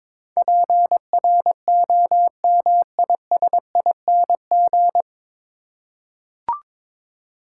Morse code, heard as a keyed tone, about 22 wpm, 700 Hz. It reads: PROMISING
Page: Morse Code Ninja practice 22wpm